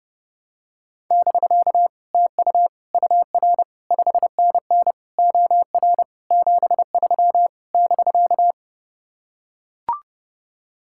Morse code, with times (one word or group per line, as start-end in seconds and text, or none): 1.1-1.9 <BK>
2.1-2.7 TU
2.9-3.6 UR
3.9-4.9 5NN
5.2-6.0 OR
6.3-7.5 73
7.7-8.5 <BK>